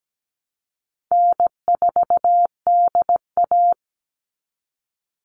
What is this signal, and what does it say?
Morse code, heard as a keyed tone, about 17 wpm, 700 Hz. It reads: N4DA
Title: NA SOTA Chaser Callsigns 17wpm